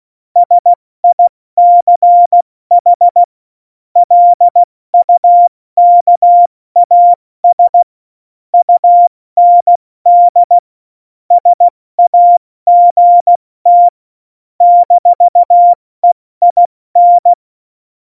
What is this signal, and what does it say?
Morse code, heard as a keyed tone, about 16 wpm, 700 Hz. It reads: SICH LUKAS UND SAGT -EIN
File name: Morsekurs / Witze / Witz 19. November 2019, 70 F80